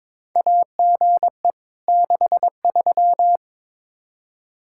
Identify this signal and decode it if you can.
Morse code, heard as a keyed tone, about 22 wpm, 700 Hz. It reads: AGE 63
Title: Practice QSO - Exchange 3 - More Info - 22wpm